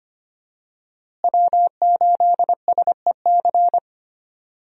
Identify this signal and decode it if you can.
Morse code, heard as a keyed tone, about 25 wpm, 700 Hz. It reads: W8SEC
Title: USA Call Signs - 5 Characters - Repeated 2x - 25wpm